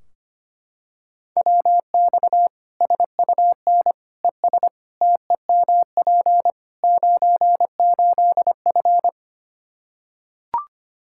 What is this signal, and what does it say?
Morse code, heard as a keyed tone, about 25 wpm, 700 Hz. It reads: WX SUN ES TEMP 98F
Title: K1USN SST QSO Practice 25wpm